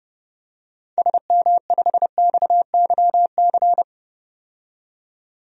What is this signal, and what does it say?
Morse code, heard as a keyed tone, about 30 wpm, 700 Hz. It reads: SM5XYC
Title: DXCC-100 Call Signs - 6 Characters - 30wpm